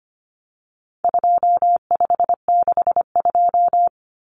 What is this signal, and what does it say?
Morse code, heard as a keyed tone, about 25 wpm, 700 Hz. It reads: 2562
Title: Common Cut Numbers - 4-Digit 25wpm